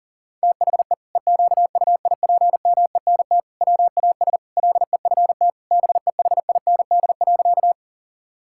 Morse code, heard as a keyed tone, about 40 wpm, 700 Hz. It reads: THE EQUIPMENT WAS LEFT BEHIND.
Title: Sentences from Top 1000 Words - Review Part 5 - 40wpm